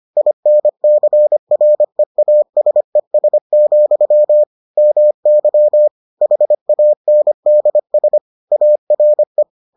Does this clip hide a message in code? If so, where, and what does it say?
Yes: Morse code from 0.2 to 9.4 s, heard as a keyed tone, about 25 wpm, 600 Hz. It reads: INCREASES, MY HANDS ARE